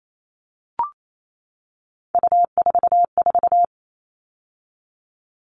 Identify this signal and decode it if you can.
Morse code, heard as a keyed tone, about 28 wpm, 700 Hz. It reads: U44
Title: All Cut Numbers - 3-Digit 28wpm